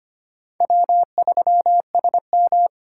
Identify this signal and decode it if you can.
Morse code, heard as a keyed tone, about 31 wpm, 700 Hz. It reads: W3SM
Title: Speed Racing - USA Call Signs 25wpm